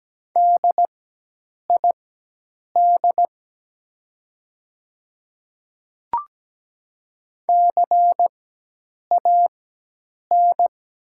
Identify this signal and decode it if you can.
Morse code, heard as a keyed tone, about 17 wpm, spaced as for 8 wpm, 700 Hz. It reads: DID CAN
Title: 4x Character Spacing - Top 300 Words - 17wpm